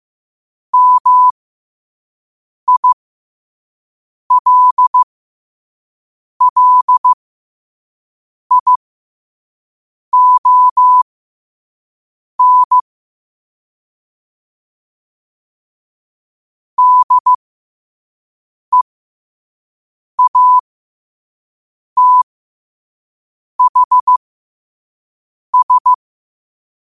Morse code, heard as a keyed tone, about 15 wpm, spaced as for 5 wpm, 1000 Hz. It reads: MILLION DEATHS
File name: 5wpm